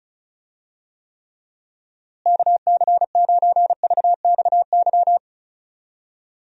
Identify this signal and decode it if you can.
Morse code, heard as a keyed tone, about 34 wpm, 700 Hz. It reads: KC9VXY
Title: USA Call Signs - 6 Characters - Repeated 2x - 35wpm